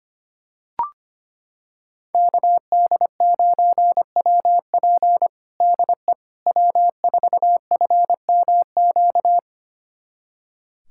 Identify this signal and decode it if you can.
Morse code, heard as a keyed tone, about 25 wpm, 700 Hz. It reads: KD9WP DE W4FMQ
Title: USA Callsign DE Callsign - 5 Characters - 25wpm